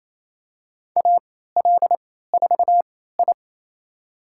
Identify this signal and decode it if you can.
Morse code, heard as a keyed tone, about 28 wpm, spaced as for 16 wpm, 700 Hz. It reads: AL4I
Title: USA Call Signs - 4 Characters - 3x Character Spacing - 28wpm